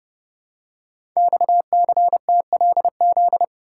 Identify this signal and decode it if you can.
Morse code, heard as a keyed tone, about 30 wpm, 700 Hz. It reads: XCTLZ
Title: Code Groups - 5 Letters - 30wpm